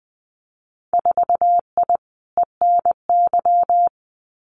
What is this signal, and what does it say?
Morse code, heard as a keyed tone, about 20 wpm, 700 Hz. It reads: 4I ENY
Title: Winter Field Day Exchanges 20wpm